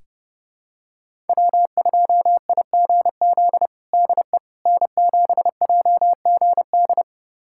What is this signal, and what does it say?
Morse code, heard as a keyed tone, about 30 wpm, 700 Hz. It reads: W2IGZ DE N7JGD
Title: USA Callsign DE Callsign - 5 Characters - 30wpm